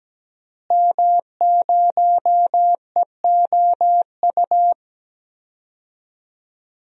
Morse code, heard as a keyed tone, about 17 wpm, 700 Hz. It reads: M0EOU
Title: DXCC-100 Call Signs - 5 Characters - Repeated 3x - 17wpm